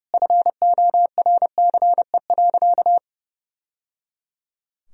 Morse code, heard as a keyed tone, about 30 wpm, 700 Hz. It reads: FORCE.
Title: Sentences from Top 1000 Words - Review Part 4 - 30wpm